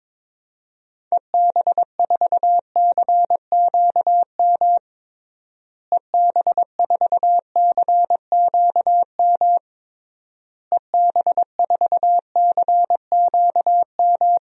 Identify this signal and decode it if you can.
Morse code, heard as a keyed tone, about 21 wpm, 700 Hz. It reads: EB4CQM EB4CQM EB4CQM
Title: DXCC-100 Call Signs - 6 Characters - Repeated 3x - 22wpm